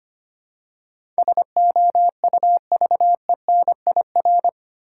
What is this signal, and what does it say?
Morse code, heard as a keyed tone, about 25 wpm, 700 Hz. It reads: SOUVENIR
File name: practice / 25wpm / Difficult Words with Hints - 25wpm